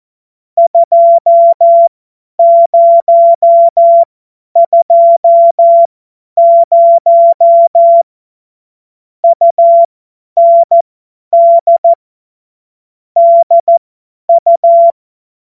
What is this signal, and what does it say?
Morse code, heard as a keyed tone, about 14 wpm, spaced as for 10 wpm, 680 Hz. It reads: 2020 UND DU
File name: Morsekurs / Witze / Witz 29. Juni 2021, 50 F70